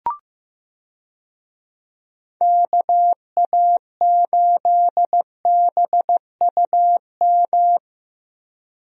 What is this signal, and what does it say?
Morse code, heard as a keyed tone, about 15 wpm, 700 Hz. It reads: KA8BUM
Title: USA Call Signs - 6 Characters - Repeated 3x - 15wpm